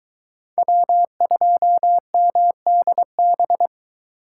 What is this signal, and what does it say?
Morse code, heard as a keyed tone, about 23 wpm, 700 Hz. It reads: W2MDB